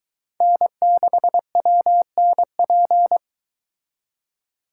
Morse code, heard as a keyed tone, about 22 wpm, 700 Hz. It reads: N6WNP